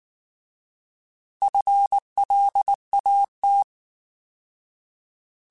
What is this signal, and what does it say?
Morse code, heard as a keyed tone, about 19 wpm, 800 Hz. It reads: FLAT
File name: Alphabetical / Alphabetical mp3s / Fs @18